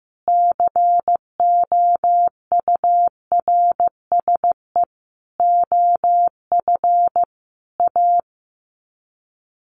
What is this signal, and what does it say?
Morse code, heard as a keyed tone, about 15 wpm, 700 Hz. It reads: COURSE OF A
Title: Sets of 5 Words Encore2 15wpm